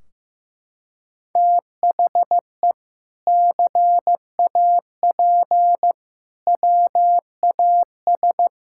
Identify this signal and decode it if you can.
Morse code, heard as a keyed tone, about 15 wpm, 700 Hz. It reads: THE CAP WAS
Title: Sentences from Top 1600 Words 15wpm